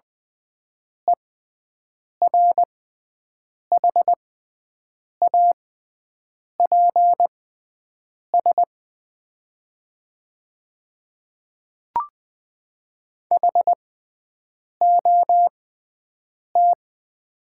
Morse code, heard as a keyed tone, about 20 wpm, spaced as for 7 wpm, 700 Hz. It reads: ERHAPS HOT
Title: 6x Character Spacing - Top 500 Words - 20wpm